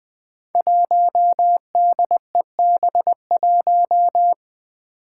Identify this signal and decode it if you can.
Morse code, heard as a keyed tone, about 21 wpm, 700 Hz. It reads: 1DEB1
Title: Five Challenging Letter-Number Combinations 20wpm